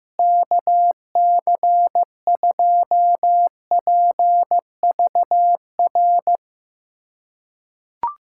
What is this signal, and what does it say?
Morse code, heard as a keyed tone, about 16 wpm, 700 Hz. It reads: KC2PVR
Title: USA Callsign DE Callsign - 6 Characters - 15wpm